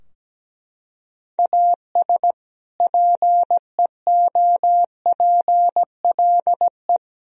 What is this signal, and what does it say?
Morse code, heard as a keyed tone, about 17 wpm, 700 Hz. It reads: AS PEOPLE
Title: Sentences from Top 1000 Words - Review Part 4 - 17wpm